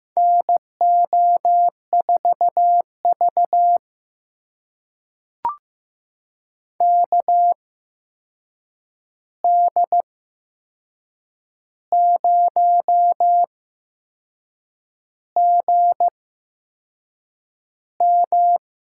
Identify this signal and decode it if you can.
Morse code, heard as a keyed tone, about 15 wpm, spaced as for 6 wpm, 700 Hz. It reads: NO4V KD0GM
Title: USA Call Signs - 8x Character Spacing - 15wpm